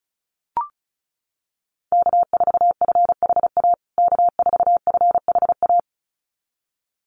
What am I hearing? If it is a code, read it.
Morse code, heard as a keyed tone, about 34 wpm, 700 Hz. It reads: K4FHA K4FHA
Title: USA Call Signs Repeated 2x 35wpm